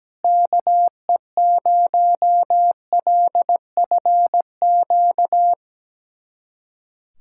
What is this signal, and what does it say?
Morse code, heard as a keyed tone, about 17 wpm, 700 Hz. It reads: KE0LFQ
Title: USA Callsign DE Callsign 17wpm